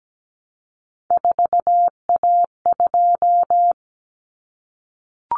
Morse code, heard as a keyed tone, about 17 wpm, 700 Hz. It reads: 4A2
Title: Common Cut Numbers - 3-Digit 17wpm